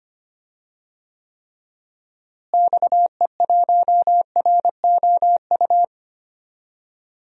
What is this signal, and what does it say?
Morse code, heard as a keyed tone, about 25 wpm, 700 Hz. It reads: XE1ROU
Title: DXCC-100 Call Signs - 6 Characters - Repeated 3x - 25wpm